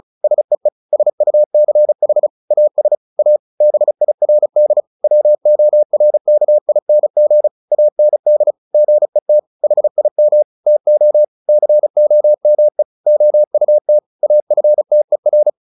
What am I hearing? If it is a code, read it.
Morse code, heard as a keyed tone, about 34 wpm, 600 Hz. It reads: SEE SUCH AS A BIRD WORKING AND GET HIM TO COME OUT AFTER
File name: the old man and the sea / the old man and the sea 1 35wpm